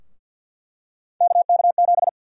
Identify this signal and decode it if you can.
Morse code, heard as a keyed tone, about 48 wpm, 700 Hz. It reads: KK7
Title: USA Call Sign Prefixes - Repeated 2x - 50wpm